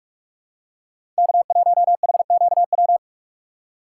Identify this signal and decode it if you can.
Morse code, heard as a keyed tone, about 46 wpm, 700 Hz. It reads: K1HQW
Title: USA Call Signs - 5 Characters - 4x Character Spacing - 45wpm